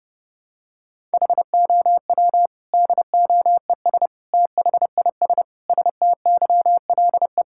Morse code, heard as a keyed tone, about 30 wpm, 700 Hz. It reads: HOW DOES THIS STYLE
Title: Sentences from Top 1000 Words - Review Part 5 - 30wpm